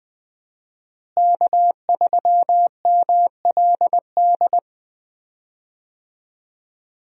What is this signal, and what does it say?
Morse code, heard as a keyed tone, about 21 wpm, 700 Hz. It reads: K3MLD
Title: USA Call Signs - 5 Characters - Repeated 2x - 20wpm